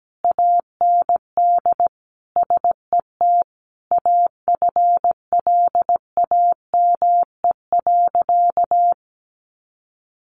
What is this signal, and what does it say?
Morse code, heard as a keyed tone, about 17 wpm, 700 Hz. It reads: AND SET AFLAME.
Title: A Sound Of Thunder 17wpm